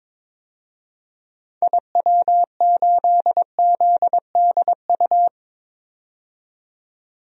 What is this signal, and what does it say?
Morse code, heard as a keyed tone, about 22 wpm, 700 Hz. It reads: IW8ZDU
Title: DXCC-100 Call Signs - 6 Characters - 22wpm